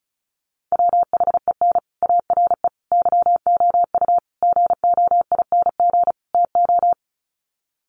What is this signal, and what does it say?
Morse code, heard as a keyed tone, about 34 wpm, 700 Hz. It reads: WHEN ARE YOU GOING TO
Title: Sets of 5 Words Encore2 35wpm